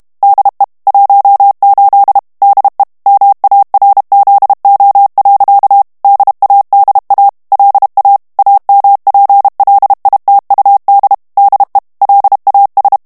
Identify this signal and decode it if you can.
Morse code, heard as a keyed tone, about 32 wpm, 800 Hz. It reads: DE 18 DE MARZO. DADA LA AMPLITUD DE LAS
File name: ESPAGNOL TEXTE EN CLAIR 2 VITESSE 30 MOTS MINUTES